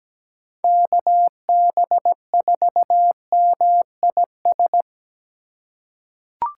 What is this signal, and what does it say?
Morse code, heard as a keyed tone, about 17 wpm, 700 Hz. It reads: KB4MIS
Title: USA Call Signs - 6 Characters - 2x Character Spacing - 17wpm